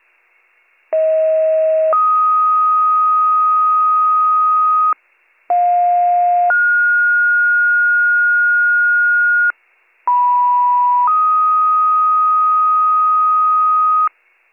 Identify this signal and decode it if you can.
Two-tone page, heard as a one-second tone, then a three-second tone, about 640 then 1250 Hz, about 710 then 1430 Hz, about 980 then 1250 Hz. It is heard from 0.9 to 14.1 s.